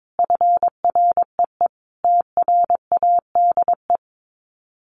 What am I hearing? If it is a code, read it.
Morse code, heard as a keyed tone, about 22 wpm, 700 Hz. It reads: FREE TRADE
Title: Sets of 5 Words Encore 22wpm